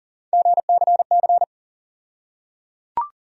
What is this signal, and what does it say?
Morse code, heard as a keyed tone, about 37 wpm, 700 Hz. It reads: GCC
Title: USA Call Sign Postfixes - 3 Letters 40wpm